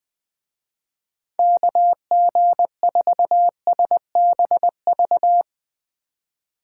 Morse code, heard as a keyed tone, about 21 wpm, 700 Hz. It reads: KG4SBV
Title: Speed Racing - USA 2x3 Call Signs 20wpm